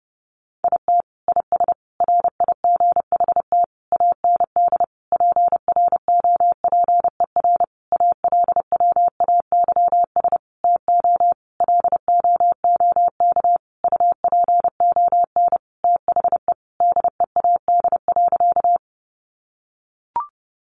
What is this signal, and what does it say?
Morse code, heard as a keyed tone, about 30 wpm, 700 Hz. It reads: IT IS RIGHT AND PROPER ALWAYS TO LOOK UPON THE DEAD.
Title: Sentences from Top 1500 Words 30wpm